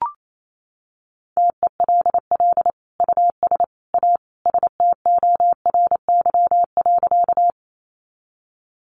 Morse code, heard as a keyed tone, about 28 wpm, 700 Hz. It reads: TELL US A STORY.
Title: Sentences from Top 300 Words 28wpm